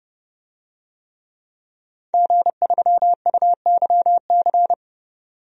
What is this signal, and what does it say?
Morse code, heard as a keyed tone, about 30 wpm, 700 Hz. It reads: G3UYC
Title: DXCC-100 Call Signs - 5 Characters - 30wpm